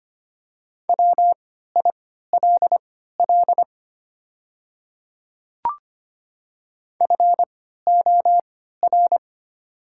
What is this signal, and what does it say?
Morse code, heard as a keyed tone, about 25 wpm, spaced as for 14 wpm, 700 Hz. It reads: WILL FOR
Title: 3x Character Spacing - Top 100 Words - 25wpm